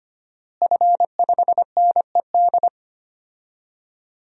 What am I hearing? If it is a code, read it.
Morse code, heard as a keyed tone, about 25 wpm, 700 Hz. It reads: F5NED
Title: DXCC-100 Call Signs - 5 Characters - Repeated 3x - 25wpm